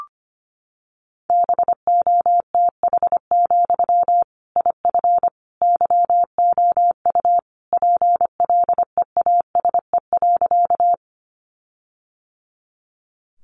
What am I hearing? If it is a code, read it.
Morse code, heard as a keyed tone, about 25 wpm, 700 Hz. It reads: BOTH, IF YOU PLEASE.
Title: Sentences from Top 1500 Words 25wpm